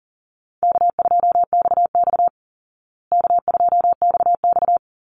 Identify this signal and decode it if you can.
Morse code, heard as a keyed tone, about 40 wpm, 700 Hz. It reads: K2XX K2XX
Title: Easy Call Signs Repeated 3x - 40wpm